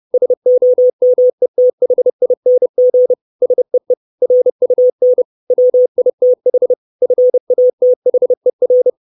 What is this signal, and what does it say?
Morse code, heard as a keyed tone, about 30 wpm, 500 Hz. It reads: SOMETHING SEE RUN WITH FATHER
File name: 632 Words 30WPM